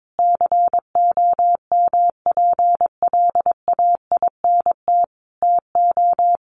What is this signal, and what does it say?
Morse code, heard as a keyed tone, about 22 wpm, 700 Hz. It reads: COMPLAINT TO
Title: Sentences from Top 1400 Words 22wpm